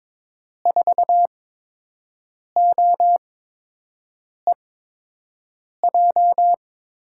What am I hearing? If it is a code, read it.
Morse code, heard as a keyed tone, about 22 wpm, spaced as for 6 wpm, 700 Hz. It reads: 4OEJ
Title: USA Call Signs - 8x Character Spacing - 22wpm